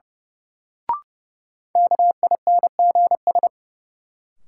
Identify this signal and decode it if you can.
Morse code, heard as a keyed tone, about 30 wpm, 700 Hz. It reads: KINGS
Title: Sentences from Top 1000 Words - Review Part 6 - New Word Drill - 30wpm